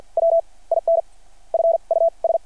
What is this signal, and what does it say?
Morse code, heard as a keyed tone, about 52 wpm, spaced as for 33 wpm, 660 Hz. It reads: WIN FUS
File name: archived / CRSnet